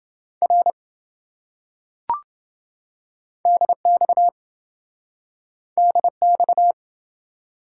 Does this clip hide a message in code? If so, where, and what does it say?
Yes: Morse code from 0.4 to 6.7 s, heard as a keyed tone, about 26 wpm, 700 Hz. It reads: R DX DX